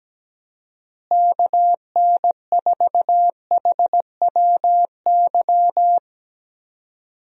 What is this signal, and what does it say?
Morse code, heard as a keyed tone, about 17 wpm, 700 Hz. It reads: KN4HWY
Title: USA Call Signs - 6 Characters - Repeated 3x - 17wpm